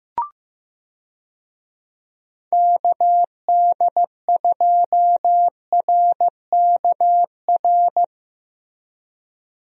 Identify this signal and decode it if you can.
Morse code, heard as a keyed tone, about 15 wpm, 700 Hz. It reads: KD2RKR
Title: USA Call Signs - 6 Characters - Repeated 2x - 15wpm